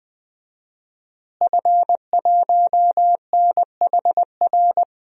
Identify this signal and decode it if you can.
Morse code, heard as a keyed tone, about 20 wpm, 700 Hz. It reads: F1NHR
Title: DXCC-100 Call Signs - 5 Characters - Repeated 2x - 20wpm